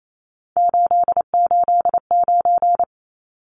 Morse code, heard as a keyed tone, about 28 wpm, 700 Hz. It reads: 889
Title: Numbers - 3-Digit 28wpm